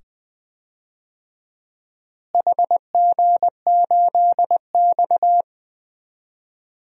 Morse code, heard as a keyed tone, about 20 wpm, 700 Hz. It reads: HG8X
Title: DXCC-100 Call Signs - 4 Characters - Repeated 2x - 20wpm